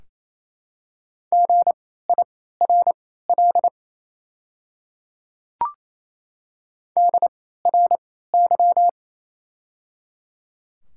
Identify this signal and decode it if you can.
Morse code, heard as a keyed tone, about 28 wpm, spaced as for 16 wpm, 700 Hz. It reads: GIRL DRY
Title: 3x Character Spacing - Top 500 Words - 28wpm